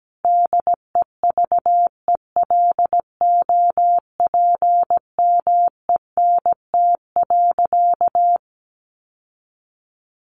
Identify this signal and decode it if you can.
Morse code, heard as a keyed tone, about 17 wpm, 700 Hz. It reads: DEVELOPMENT.
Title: Sentences from Top 1400 Words 17wpm